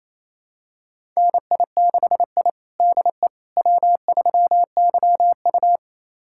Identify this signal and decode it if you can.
Morse code, heard as a keyed tone, about 28 wpm, 700 Hz. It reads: NI6I DE W3YU
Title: USA Callsign DE Callsign - 4 Characters - 28wpm